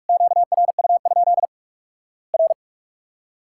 Morse code, heard as a keyed tone, about 46 wpm, 700 Hz. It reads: QRU? R
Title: Worked All Europe DX Contest - QTC Exchange - 45wpm